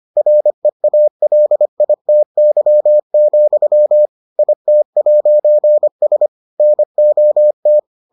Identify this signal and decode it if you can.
Morse code, heard as a keyed tone, about 25 wpm, 600 Hz. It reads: REALITY, IT'S NOT